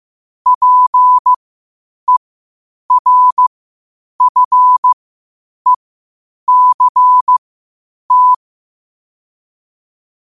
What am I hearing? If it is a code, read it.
Morse code, heard as a keyed tone, about 15 wpm, spaced as for 8 wpm, 1000 Hz. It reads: PERFECT
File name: qotdfiles / 8wpm